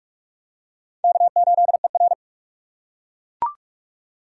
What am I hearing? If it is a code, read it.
Morse code, heard as a keyed tone, about 44 wpm, 690 Hz. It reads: K8ER
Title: USA Call Signs - 3x Character Spacing - 45wpm